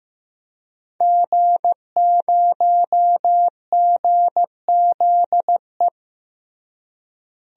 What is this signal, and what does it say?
Morse code, heard as a keyed tone, about 15 wpm, 700 Hz. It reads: G0GZE
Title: DXCC-100 Call Signs - 5 Characters - Repeated 3x - 15wpm